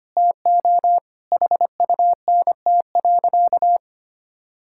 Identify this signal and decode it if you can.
Morse code, heard as a keyed tone, about 25 wpm, 700 Hz. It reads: TO HUNT.
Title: Sentences from Top 1600 Words 25wpm